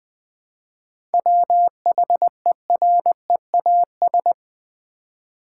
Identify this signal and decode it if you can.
Morse code, heard as a keyed tone, about 20 wpm, 700 Hz. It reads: WHEREAS
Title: Seven Letter Words - 20wpm